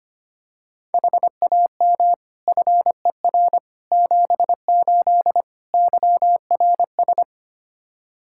Morse code, heard as a keyed tone, about 25 wpm, 700 Hz. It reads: HAM FER 78 YRS
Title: Practice QSO - Exchange 3 - More Info - 25wpm